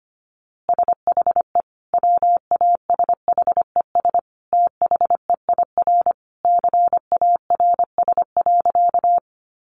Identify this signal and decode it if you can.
Morse code, heard as a keyed tone, about 25 wpm, 700 Hz. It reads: SHE WASHES THEIR CARS.
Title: Easy Phrases 25wpm